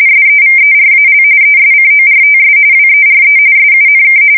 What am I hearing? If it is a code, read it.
RTTY, about 45.45 baud. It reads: SEVERAL ESCORTS WERE SUNK
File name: fsk170 / USS-New-Jersey